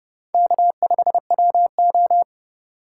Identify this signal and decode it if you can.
Morse code, heard as a keyed tone, about 30 wpm, 700 Hz. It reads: K5WO
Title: USA Call Signs - 4 Characters - 2x Character Spacing - 30wpm